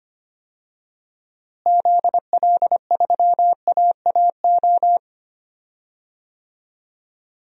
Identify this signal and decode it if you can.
Morse code, heard as a keyed tone, about 25 wpm, 700 Hz. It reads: ZL3AAO
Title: DXCC-100 Call Signs - 6 Characters - Repeated 2x - 25wpm